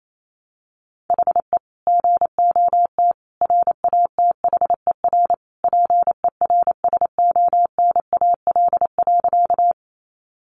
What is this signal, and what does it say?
Morse code, heard as a keyed tone, about 28 wpm, 700 Hz. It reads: HE GOT RATHER PERSONAL.
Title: Sentences from Top 700 Words 28wpm